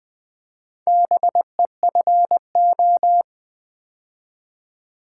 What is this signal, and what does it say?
Morse code, heard as a keyed tone, about 20 wpm, 700 Hz. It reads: BEFO
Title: Build A Word - 2 Letters At A Time - 6-Letter Words - 20wpm